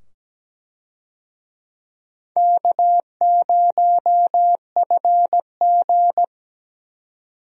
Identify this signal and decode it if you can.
Morse code, heard as a keyed tone, about 17 wpm, 700 Hz. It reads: K0FG
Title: USA Call Signs - 4 Characters - 17wpm